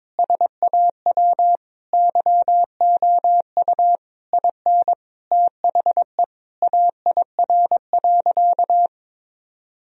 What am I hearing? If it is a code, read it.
Morse code, heard as a keyed tone, about 22 wpm, 700 Hz. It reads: SAW YOU IN THE AIR.
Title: Sentences from Top 1000 Words - Review Part 1 - 22wpm